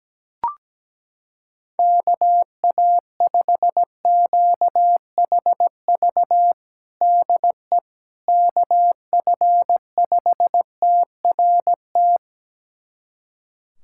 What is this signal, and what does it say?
Morse code, heard as a keyed tone, about 17 wpm, 700 Hz. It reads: KA5QHV DE KF5TRT
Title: USA Callsign DE Callsign - 6 Characters - 17wpm